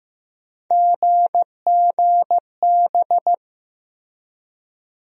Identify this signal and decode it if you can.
Morse code, heard as a keyed tone, about 15 wpm, 700 Hz. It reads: GGB
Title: USA Call Sign Postfixes - 3 Letters 15wpm